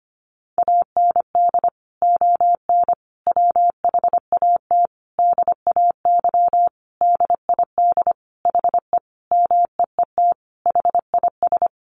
Morse code, heard as a keyed tone, about 25 wpm, 700 Hz. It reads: AND ON WHAT DAY DID HE MEET HIS
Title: Sentences from Top 1100 Words 25wpm